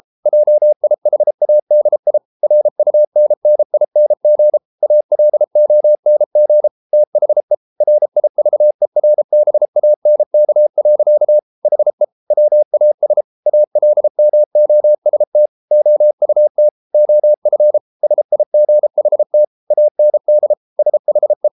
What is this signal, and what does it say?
Morse code, heard as a keyed tone, about 33 wpm, 600 Hz. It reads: JIHADI RUNNING ALONG THE RIVERBANK. HE WAS ALMOST OUT OF SIGHT AND SHE